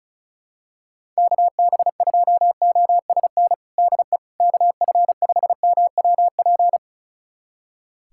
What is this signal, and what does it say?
Morse code, heard as a keyed tone, about 34 wpm, 700 Hz. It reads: KB2OSN DE KF5MWP
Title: USA Callsign DE Callsign 35wpm